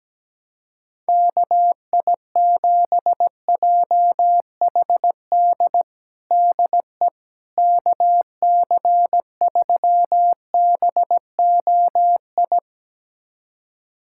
Morse code, heard as a keyed tone, about 17 wpm, 700 Hz. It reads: KI7JHD DE KC3BOI
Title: USA Callsign DE Callsign 17wpm